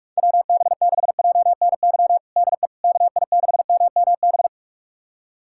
Morse code, heard as a keyed tone, about 46 wpm, 700 Hz. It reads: WB6JNY DE KI6MMB
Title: USA Callsign DE Callsign 45wpm